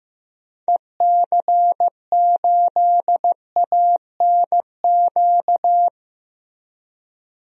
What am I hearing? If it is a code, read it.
Morse code, heard as a keyed tone, about 15 wpm, 700 Hz. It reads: EC8ANQ
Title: DXCC-100 Call Signs - 6 Characters - Repeated 2x - 15wpm